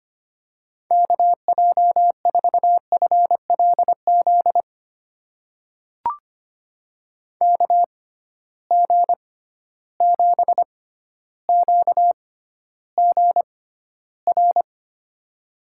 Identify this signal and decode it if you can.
Morse code, heard as a keyed tone, about 25 wpm, spaced as for 13 wpm, 700 Hz. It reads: KJ4FLZ KG7QGR
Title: USA Call Signs - 6 Characters - 6x Character Spacing - 25wpm